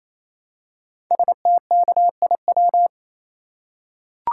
Morse code, heard as a keyed tone, about 27 wpm, 700 Hz. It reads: STKIW
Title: Code Groups - 5 Letters - 28wpm